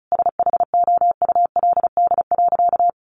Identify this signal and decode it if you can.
Morse code, heard as a keyed tone, about 34 wpm, 700 Hz. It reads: SHOULD.
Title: Sentences from Top 700 Words 35wpm